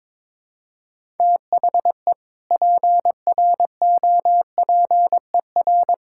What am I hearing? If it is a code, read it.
Morse code, heard as a keyed tone, about 21 wpm, 700 Hz. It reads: THE PROPER
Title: Sentences from Top 1600 Words 22wpm